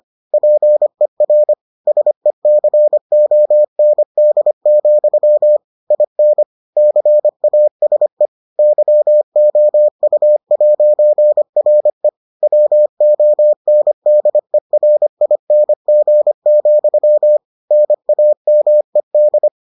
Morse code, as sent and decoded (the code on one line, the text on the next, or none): .--. .-. / ... -.-. --- -. -.. --..-- / .. -. / -.-. .- ... / -.-- --- ..- .----. .-. / .-- --- -. -.. .-. .. -. --. --..-- / -. .- -- . -..
PER SECOND, IN CASE YOU'RE WONDERING, NAMED